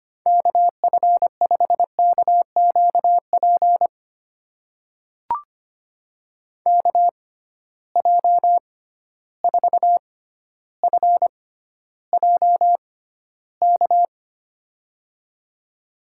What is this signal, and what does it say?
Morse code, heard as a keyed tone, about 25 wpm, spaced as for 13 wpm, 700 Hz. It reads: KF5KQP KJ4FJK
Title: USA Call Signs - 6 Characters - 6x Character Spacing - 25wpm